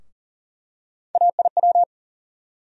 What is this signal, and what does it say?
Morse code, heard as a keyed tone, about 40 wpm, 700 Hz. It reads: AIW
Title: USA Call Sign Postfixes - 3 Letters 40wpm